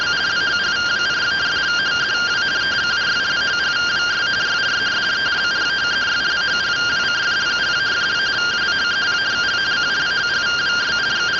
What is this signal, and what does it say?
Two-tone teleprinter signal, about 170 Hz shift.